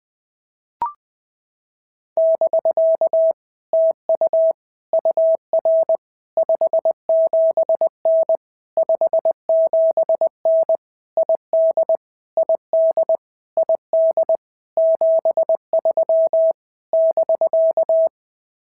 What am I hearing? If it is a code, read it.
Morse code, heard as a keyed tone, about 20 wpm, 650 Hz. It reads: <BK> TU UR 57N 57N ID ID ID 73 <BK>
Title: POTA QSO Practice - 20wpm